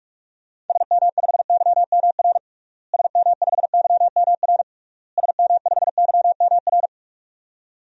Morse code, heard as a keyed tone, about 41 wpm, 700 Hz. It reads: SM5YMR SM5YMR SM5YMR
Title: DXCC-100 Call Signs - 6 Characters - Repeated 3x - 45wpm